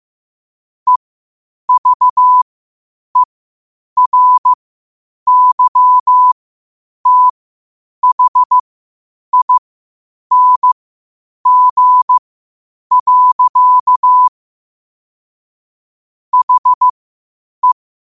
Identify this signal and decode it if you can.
Morse code, heard as a keyed tone, about 15 wpm, spaced as for 8 wpm, 1000 Hz. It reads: EVERYTHING. HE
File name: qotdfiles / 8wpm